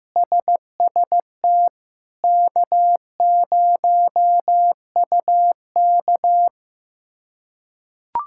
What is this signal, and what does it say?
Morse code, heard as a keyed tone, about 15 wpm, 700 Hz. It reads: SST K0UK
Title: K1USN SST QSO Practice - 15wpm